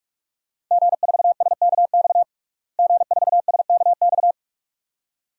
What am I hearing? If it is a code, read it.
Morse code, heard as a keyed tone, about 42 wpm, 700 Hz. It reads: G4SKX G4SKX
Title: DXCC-100 Call Signs - 5 Characters - Repeated 2x - 45wpm